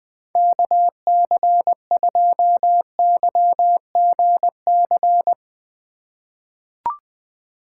Morse code, heard as a keyed tone, about 20 wpm, 700 Hz. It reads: KC2YGC
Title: USA Call Signs - 6 Characters - 4x Character Spacing - 20wpm